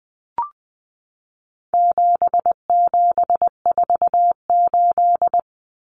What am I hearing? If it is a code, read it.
Morse code, heard as a keyed tone, about 20 wpm, 700 Hz. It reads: 7748
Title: Common Cut Numbers - 4-Digit 20wpm